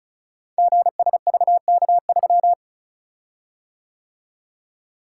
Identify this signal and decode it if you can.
Morse code, heard as a keyed tone, about 36 wpm, 700 Hz. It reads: GSVK3
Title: Code Groups - 5 Letter:Numbers - 35wpm